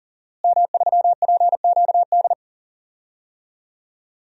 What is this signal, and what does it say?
Morse code, heard as a keyed tone, about 40 wpm, 700 Hz. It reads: M3PQD
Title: DXCC-100 Call Signs - 5 Characters - Repeated 3x - 40wpm